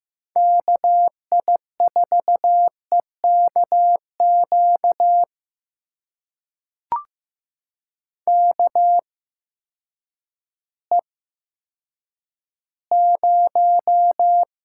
Morse code, heard as a keyed tone, about 15 wpm, spaced as for 9 wpm, 700 Hz. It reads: KI4EKQ KE0
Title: USA Call Signs - 6 Characters - 8x Character Spacing - 15wpm